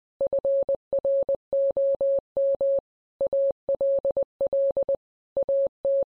Morse code, heard as a keyed tone, about 20 wpm, 550 Hz. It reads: FROM ALL AT